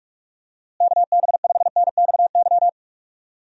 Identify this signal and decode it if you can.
Morse code, heard as a keyed tone, about 41 wpm, 700 Hz. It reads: KB5NXY
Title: USA Call Signs - 6 Characters - Repeated 3x - 45wpm